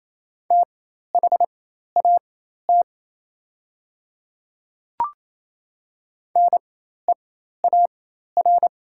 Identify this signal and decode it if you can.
Morse code, heard as a keyed tone, about 28 wpm, spaced as for 13 wpm, 690 Hz. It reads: THAT NEAR
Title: 4x Character Spacing - Top 200 Words - 28wpm